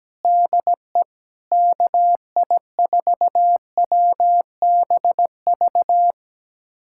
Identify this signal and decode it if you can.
Morse code, heard as a keyed tone, about 17 wpm, 700 Hz. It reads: DE KI4WBV
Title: USA Callsign DE Callsign - 6 Characters - 17wpm